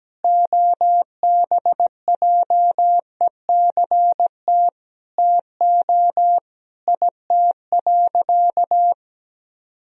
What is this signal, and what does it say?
Morse code, heard as a keyed tone, about 17 wpm, 700 Hz. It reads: OBJECT TO IT.
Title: Sentences from Top 1000 Words - Review Part 3 - 17wpm